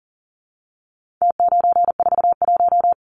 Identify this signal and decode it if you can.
Morse code, heard as a keyed tone, about 40 wpm, 700 Hz. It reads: T941
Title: Numbers - Contest Style - 4-Digit 40wpm